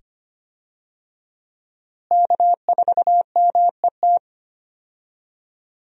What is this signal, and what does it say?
Morse code, heard as a keyed tone, about 25 wpm, 700 Hz. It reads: K4MET
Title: USA Call Signs - 5 Characters - Repeated 2x - 25wpm